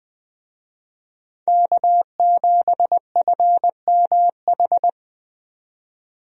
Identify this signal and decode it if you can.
Morse code, heard as a keyed tone, about 20 wpm, 700 Hz. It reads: K7FMH
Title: USA Call Signs - 5 Characters - 20wpm